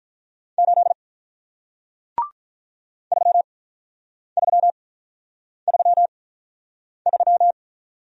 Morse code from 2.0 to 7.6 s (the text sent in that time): 3333